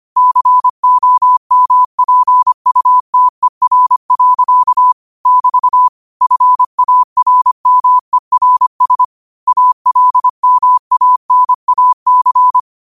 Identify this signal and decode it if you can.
Morse code, heard as a keyed tone, about 25 wpm, 1000 Hz. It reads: COMPUTER. <BT> FARMERS ALMANAC
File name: qotdfiles / 25wpm